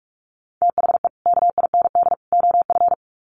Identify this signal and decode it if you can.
Morse code, heard as a keyed tone, about 44 wpm, 700 Hz. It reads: THE KIND OF
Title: Sets of 3 Words 45wpm